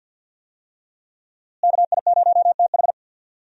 Morse code, heard as a keyed tone, about 52 wpm, 700 Hz. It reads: KI0TH
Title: USA Call Signs - 5 Characters - Repeated 3x - 50wpm